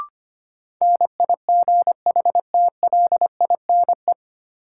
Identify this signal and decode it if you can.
Morse code, heard as a keyed tone, about 25 wpm, 700 Hz. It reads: NIGHTLINE
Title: Sets of 5 Words - New Word Drill - 25wpm